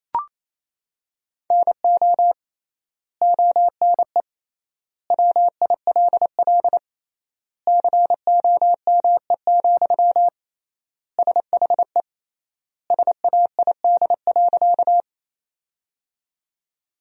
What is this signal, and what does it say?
Morse code, heard as a keyed tone, about 27 wpm, 700 Hz. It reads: NO ONE WILL COME, SHE SAID.
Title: 3x Word-Spacing - Sentences from Top 100 Words - 28wpm